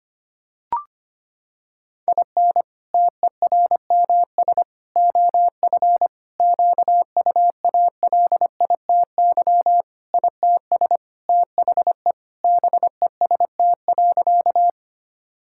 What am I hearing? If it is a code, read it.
Morse code, heard as a keyed tone, about 25 wpm, 700 Hz. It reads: IN TERMS OF QUALITY ITS THE BEST.
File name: practice / 25wpm / Sentences with 3-Word Phrases - 25wpm